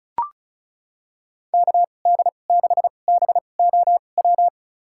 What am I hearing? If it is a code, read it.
Morse code, heard as a keyed tone, about 36 wpm, spaced as for 25 wpm, 700 Hz. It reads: KD6BOW
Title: USA Call Signs - 6 Characters - 2x Character Spacing - 35wpm